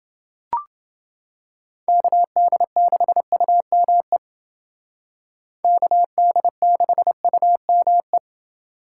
Morse code, heard as a keyed tone, about 28 wpm, 700 Hz. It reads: KD6UME KD6UME